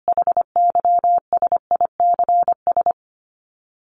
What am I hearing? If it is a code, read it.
Morse code, heard as a keyed tone, about 25 wpm, 700 Hz. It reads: HYSICS
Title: Top 501-750 Words 25wpm